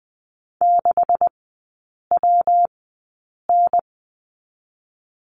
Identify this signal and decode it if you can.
Morse code, heard as a keyed tone, about 20 wpm, spaced as for 8 wpm, 700 Hz. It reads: 6WN
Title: 191 Lesson 31e - Letter v - Three Characters 20wpm